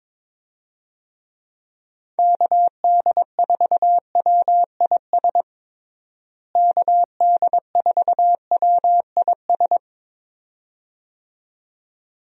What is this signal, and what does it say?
Morse code, heard as a keyed tone, about 22 wpm, 700 Hz. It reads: KD4WIS KD4WIS
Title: USA Call Signs - 6 Characters - Repeated 2x - 22wpm